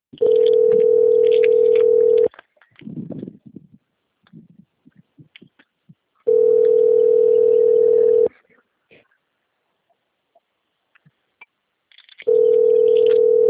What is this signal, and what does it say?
Ringback tone (North America), 440 + 480 Hz.